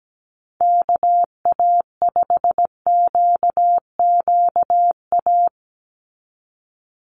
Morse code, heard as a keyed tone, about 17 wpm, 700 Hz. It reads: KA5QQA
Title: NA SOTA Chaser Callsigns 17wpm